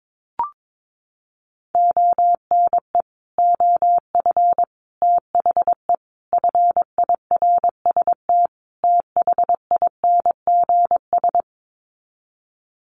Morse code, heard as a keyed tone, about 22 wpm, 700 Hz. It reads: ONE OF THE FIRST THINGS
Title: Sets of 5 Words Encore 22wpm